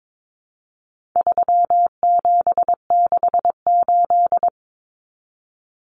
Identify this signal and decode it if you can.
Morse code, heard as a keyed tone, about 22 wpm, 700 Hz. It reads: 3768
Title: Numbers - 4-Digit 22wpm